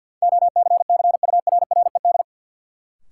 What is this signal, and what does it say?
Morse code, heard as a keyed tone, about 47 wpm, 700 Hz. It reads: OCCURRED
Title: Sets of 5 Words Encore - New Word Drill - 50wpm